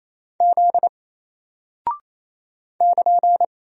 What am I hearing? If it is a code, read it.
Morse code, heard as a keyed tone, about 28 wpm, 700 Hz. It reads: Z <KN>
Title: Single Character (Letter:Number:Punctuation:Prosign) Alt 28wpm